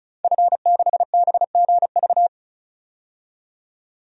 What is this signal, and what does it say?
Morse code, heard as a keyed tone, about 36 wpm, 700 Hz. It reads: F6BGV
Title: DXCC-100 Call Signs - 5 Characters - 35wpm